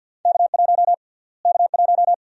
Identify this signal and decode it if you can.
Morse code, heard as a keyed tone, about 49 wpm, 700 Hz. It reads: K1 K1
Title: USA Call Sign Prefixes - Repeated 2x - 50wpm